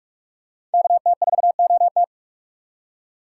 Morse code, heard as a keyed tone, about 44 wpm, 700 Hz. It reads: KT4OT
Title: USA Call Signs - 4x Character Spacing - 45wpm